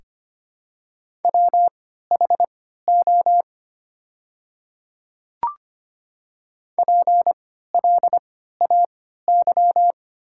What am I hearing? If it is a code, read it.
Morse code, heard as a keyed tone, about 25 wpm, spaced as for 14 wpm, 700 Hz. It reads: WHO PLAY
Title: 3x Character Spacing - Top 200 Words - 25wpm